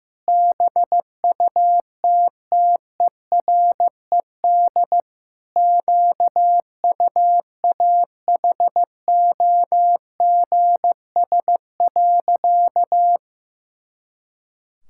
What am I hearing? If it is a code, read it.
Morse code, heard as a keyed tone, about 15 wpm, 700 Hz. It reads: BUTTERED QUAHOGS.
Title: Pangram Challenge v2 - 15wpm